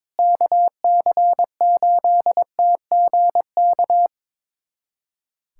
Morse code, heard as a keyed tone, about 22 wpm, 700 Hz. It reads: KC8TGK
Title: USA Callsign DE Callsign - 6 Characters - 22wpm